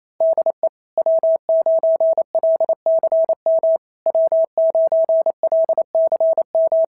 Morse code, heard as a keyed tone, about 25 wpm, 650 Hz. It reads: DE W9LCM W9LCM
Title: SOTA QSO Practice 28wpm